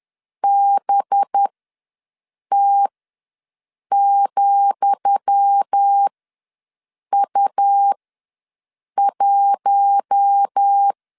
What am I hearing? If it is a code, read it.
Morse code, heard as a keyed tone, about 10 wpm, spaced as for 6 wpm, 800 Hz. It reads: BT,U1
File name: LEZIONI / 10 SIGNS / 10D COMA ws 10wpm 4314